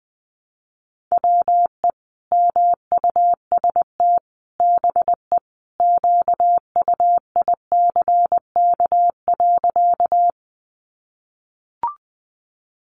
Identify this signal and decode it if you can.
Morse code, heard as a keyed tone, about 20 wpm, 700 Hz. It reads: WE MUST BE QUICK.
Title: Sentences from Top 500 Words 20wpm